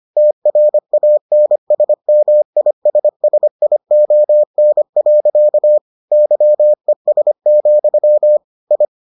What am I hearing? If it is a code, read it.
Morse code, heard as a keyed tone, about 25 wpm, 600 Hz. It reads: TRANSMISSION. YES, I